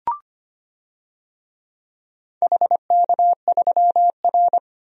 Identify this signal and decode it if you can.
Morse code, heard as a keyed tone, about 25 wpm, 700 Hz. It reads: HK3R
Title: DXCC-100 Call Signs - 4 Characters - 25wpm